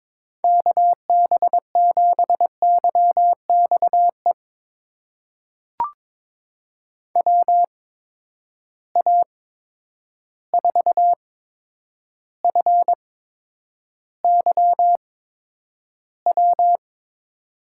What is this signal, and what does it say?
Morse code, heard as a keyed tone, about 21 wpm, spaced as for 9 wpm, 700 Hz. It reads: KB7YXE WA4FYW
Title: USA Call Signs - 6 Characters - 8x Character Spacing - 22wpm